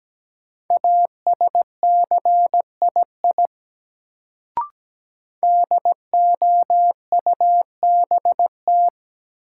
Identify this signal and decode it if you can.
Morse code, heard as a keyed tone, about 17 wpm, 700 Hz. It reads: ASCII DOUBT
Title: Sets of 5 Words - New Word Drill - 17wpm